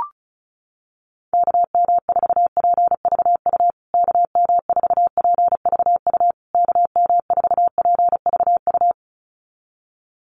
Morse code, heard as a keyed tone, about 34 wpm, 700 Hz. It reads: KM4PVU KM4PVU KM4PVU
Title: USA Call Signs Repeated 3x 35wpm